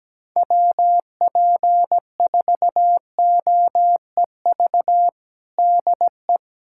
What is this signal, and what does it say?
Morse code, heard as a keyed tone, about 17 wpm, 700 Hz. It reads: WP4OEV DE
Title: USA Callsign DE Callsign - 6 Characters - 17wpm